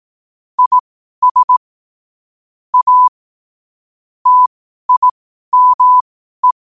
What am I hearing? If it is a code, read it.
Morse code, heard as a keyed tone, about 18 wpm, spaced as for 12 wpm, 1000 Hz. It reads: IS A TIME
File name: qotdfiles / 12wpm